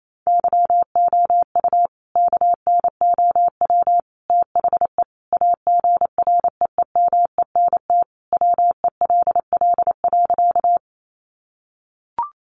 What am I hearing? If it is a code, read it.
Morse code, heard as a keyed tone, about 28 wpm, 700 Hz. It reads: YOU KNOW THE AGREEMENT WELL.
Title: Sentences from Top 1500 Words 28wpm